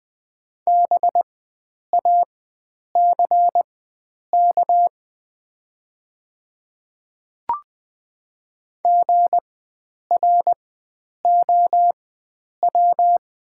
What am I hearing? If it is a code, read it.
Morse code, heard as a keyed tone, about 20 wpm, spaced as for 9 wpm, 700 Hz. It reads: BACK GROW
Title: 4x Character Spacing - Top 300 Words - 20wpm